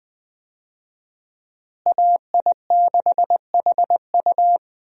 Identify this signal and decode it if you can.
Morse code, heard as a keyed tone, about 20 wpm, 700 Hz. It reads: AI6HU
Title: USA Call Signs - 5 Characters - Repeated 2x - 20wpm